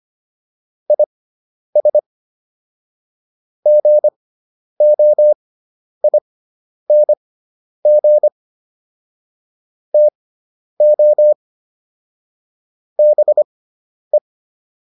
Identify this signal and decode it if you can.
Morse code, heard as a keyed tone, about 25 wpm, spaced as for 10 wpm, 600 Hz. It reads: IS GOING TO BE